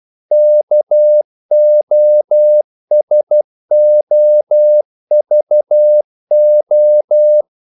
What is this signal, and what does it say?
Morse code, heard as a keyed tone, about 12 wpm, 600 Hz. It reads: KOSOVO
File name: CW / Practice / NationalCapitals12